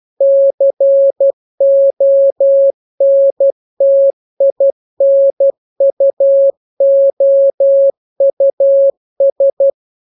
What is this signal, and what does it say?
Morse code, heard as a keyed tone, about 12 wpm, 550 Hz. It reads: CONTINUOUS
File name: jokes / joke 5 12 0000